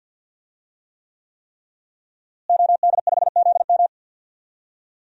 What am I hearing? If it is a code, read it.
Morse code, heard as a keyed tone, about 48 wpm, 700 Hz. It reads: OD5ZM
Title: DXCC-100 Call Signs - 5 Characters - Repeated 3x - 50wpm